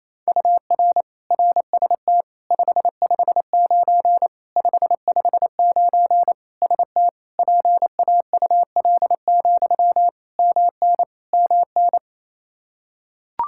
Morse code, heard as a keyed tone, about 28 wpm, 700 Hz. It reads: UR RST 559 559 ST PAUL, MN MN
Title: Practice QSO - Exchange 1 - Beginning - 28wpm